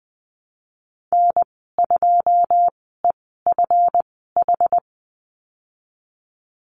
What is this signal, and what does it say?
Morse code, heard as a keyed tone, about 20 wpm, spaced as for 14 wpm, 700 Hz. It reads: N2EFH
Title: 179 Lesson 29g - Slash - Call Signs 20wpm